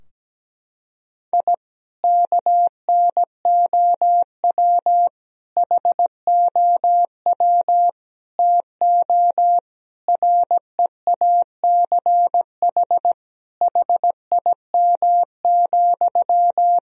Morse code, heard as a keyed tone, about 17 wpm, 700 Hz. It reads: I KNOW HOW TO REACH HIM,
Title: Sentences from Top 1000 Words - Review Part 2 - 17wpm